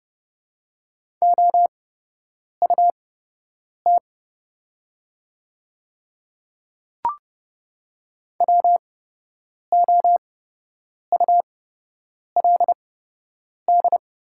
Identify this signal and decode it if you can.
Morse code, heard as a keyed tone, about 30 wpm, spaced as for 8 wpm, 700 Hz. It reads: OUT WOULD
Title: 8x Character Spacing - Top 100 Words - 30wpm